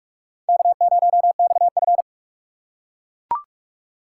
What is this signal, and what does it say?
Morse code, heard as a keyed tone, about 43 wpm, 700 Hz. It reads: K0XF